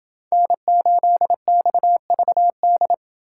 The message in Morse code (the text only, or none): N8XVD